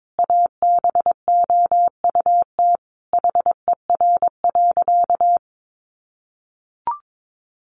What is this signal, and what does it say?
Morse code, heard as a keyed tone, about 22 wpm, 700 Hz. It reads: ABOUT HER.
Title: Sentences from Top 800 Words 22wpm